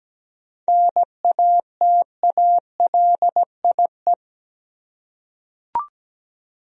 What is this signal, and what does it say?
Morse code, heard as a keyed tone, about 17 wpm, 700 Hz. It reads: NATALIE
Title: USA Common Names - 17wpm